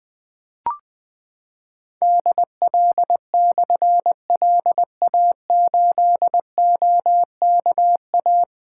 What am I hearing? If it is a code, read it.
Morse code, heard as a keyed tone, about 21 wpm, 700 Hz. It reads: DL/LA8OKA
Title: Difficult Call Signs 20wpm